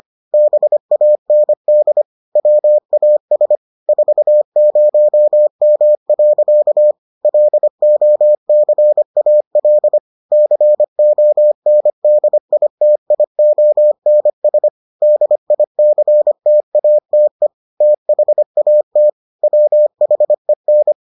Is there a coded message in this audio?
Yes: Morse code from 0.3 to 20.9 s, heard as a keyed tone, about 25 wpm, 600 Hz. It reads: BAND WAS 40M. LOCAL CONDITIONS DICTATE THAT WHEN